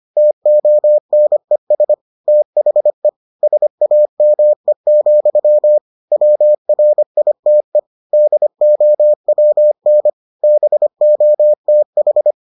Morse code, heard as a keyed tone, about 25 wpm, 600 Hz. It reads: TONES THE SAME, WRITE DOWN BOTH